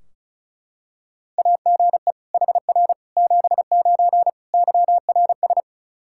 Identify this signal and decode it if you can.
Morse code, heard as a keyed tone, about 36 wpm, 700 Hz. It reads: AGE HR 79 YRS
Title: K1USN SST QSO Practice 35wpm